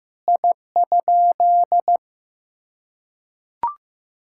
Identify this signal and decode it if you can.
Morse code, heard as a keyed tone, about 16 wpm, 700 Hz. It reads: I?
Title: Sentences from Top 1000 Words - Review Part 5 - 15wpm